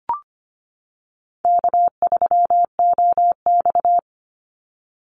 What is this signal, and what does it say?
Morse code, heard as a keyed tone, about 25 wpm, 700 Hz. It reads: K3OX
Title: Easy Call Signs - 25wpm